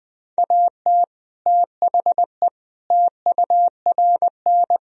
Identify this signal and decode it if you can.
Morse code, heard as a keyed tone, about 20 wpm, 700 Hz. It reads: AT THE TURN
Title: Speed-Racing - Sets of 5 Words 20wpm